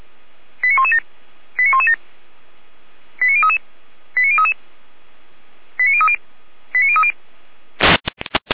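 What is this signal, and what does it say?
Five-tone selective call (ZVEI), heard as quick runs of five tones, tones of about 70 ms. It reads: 89117 89117 89033 89033 89030 89030
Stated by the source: Received on an SDR.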